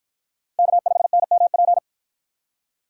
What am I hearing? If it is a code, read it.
Morse code, heard as a keyed tone, about 51 wpm, 700 Hz. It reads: K5NMP